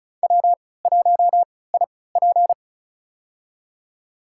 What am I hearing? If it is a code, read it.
Morse code, heard as a keyed tone, about 36 wpm, spaced as for 20 wpm, 700 Hz. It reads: W1IP
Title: USA Call Signs - 4 Characters - 3x Character Spacing - 35wpm